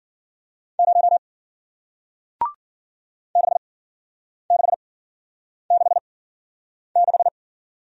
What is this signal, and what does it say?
Morse code, heard as a keyed tone, about 53 wpm, spaced as for 9 wpm, 700 Hz. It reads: 0 6666